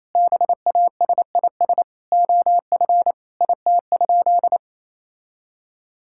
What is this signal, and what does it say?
Morse code, heard as a keyed tone, about 27 wpm, 700 Hz. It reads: BASIS OF IT?
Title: Sentences from Top 1000 Words - Review Part 5 - 28wpm